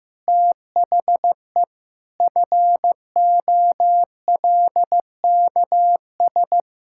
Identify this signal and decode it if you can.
Morse code, heard as a keyed tone, about 15 wpm, 700 Hz. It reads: THE FOLKS
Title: Pangram Challenge v2 - 15wpm